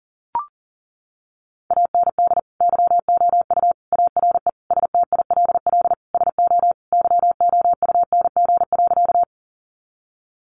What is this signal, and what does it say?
Morse code, heard as a keyed tone, about 40 wpm, 700 Hz. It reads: AND YOU ARE STILL SO YOUNG.
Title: Sentences from Top 400 Words 40wpm